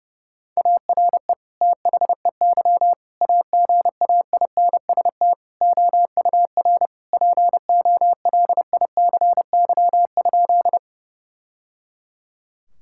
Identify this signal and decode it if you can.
Morse code, heard as a keyed tone, about 30 wpm, 700 Hz. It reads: ARE THEY AGAINST OUR POLICY?
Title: Sentences from Top 1000 Words - Review Part 5 - 30wpm